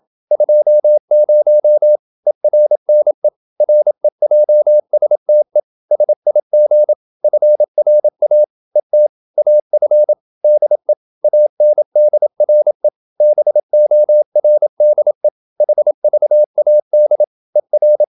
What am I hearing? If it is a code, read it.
Morse code, heard as a keyed tone, about 27 wpm, 600 Hz. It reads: 20 ERNE REJSTE SIG FRA ET AF DE ANDRE BORDE HVAD ER
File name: Ferie QRQ 27wpm